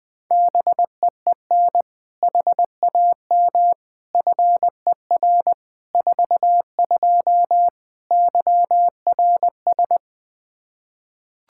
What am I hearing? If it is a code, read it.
Morse code, heard as a keyed tone, about 20 wpm, 700 Hz. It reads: BEEN HAM FER 42 YRS
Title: Practice QSO - Exchange 3 - More Info - 20wpm